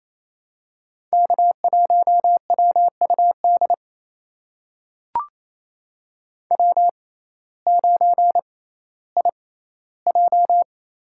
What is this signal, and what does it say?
Morse code, heard as a keyed tone, about 28 wpm, spaced as for 15 wpm, 700 Hz. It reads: K1WUD W9IJ
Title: USA Call Signs - 6x Character Spacing - 28wpm